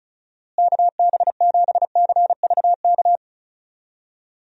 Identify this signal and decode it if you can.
Morse code, heard as a keyed tone, about 36 wpm, 700 Hz. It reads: KB7CVK
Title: USA Call Signs - 6 Characters - Repeated 2x - 35wpm